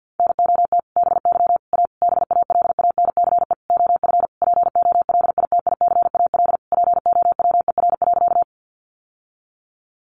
Morse code, heard as a keyed tone, about 50 wpm, 700 Hz. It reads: NOT BY A BALANCE OF POLITICAL POWER.
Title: Sentences from Top 900 Words 50wpm